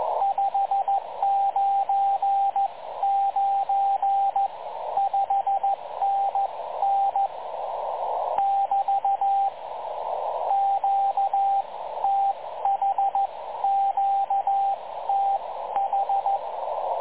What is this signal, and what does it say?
Morse code, heard as a keyed tone, about 14 wpm, 780 Hz, with radio noise. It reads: 599 5NN <BT> QTH QTH
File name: dlazaawansowanych / nagraniazpasma / websdr recording 3565.0kHz